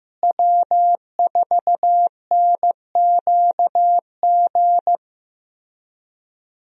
Morse code, heard as a keyed tone, about 15 wpm, 700 Hz. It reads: W4NQG
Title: USA Call Signs - 5 Characters - Repeated 2x - 15wpm